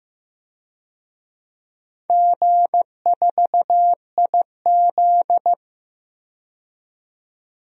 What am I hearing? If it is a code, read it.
Morse code, heard as a keyed tone, about 15 wpm, 700 Hz. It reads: G4IZ
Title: DXCC-100 Call Signs - 4 Characters - 15wpm